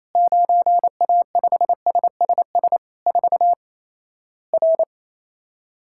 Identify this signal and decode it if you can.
Morse code, heard as a keyed tone, about 28 wpm, 700 Hz. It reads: 9A5SSS 4 R
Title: Worked All Europe DX Contest - QTC Exchange - 28wpm